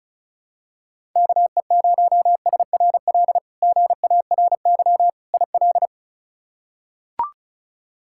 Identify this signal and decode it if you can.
Morse code, heard as a keyed tone, about 36 wpm, 700 Hz. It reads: KE0SRL GARY IL
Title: K1USN SST QSO Practice - 35wpm